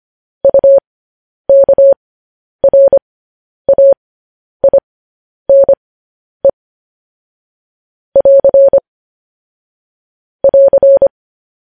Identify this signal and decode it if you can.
Morse code, heard as a keyed tone, about 25 wpm, spaced as for 10 wpm, 550 Hz. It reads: UKRAINE <AR> <AR>
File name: podcast / News headlines-1x-25x10